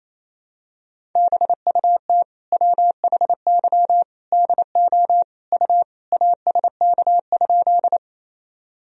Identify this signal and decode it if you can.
Morse code, heard as a keyed tone, about 28 wpm, 700 Hz. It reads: BUT WHY DO U ASK?
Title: Sentences from Top 200 Words and Abbreviations 28wpm